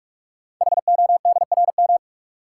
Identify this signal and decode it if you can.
Morse code, heard as a keyed tone, about 44 wpm, 690 Hz. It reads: HODRM
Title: Code Groups - 5 Letters - 45wpm